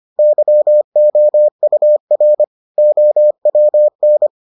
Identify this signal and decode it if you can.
Morse code, heard as a keyed tone, about 25 wpm, 600 Hz. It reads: YOUR OWN